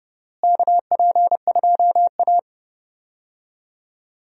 Morse code, heard as a keyed tone, about 30 wpm, 700 Hz. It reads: KP2A
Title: DXCC-100 Call Signs - 4 Characters - Repeated 3x - 30wpm